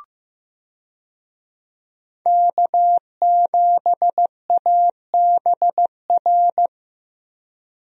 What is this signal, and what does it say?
Morse code, heard as a keyed tone, about 15 wpm, 700 Hz. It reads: K7ABR
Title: USA Call Signs - 5 Characters - 15wpm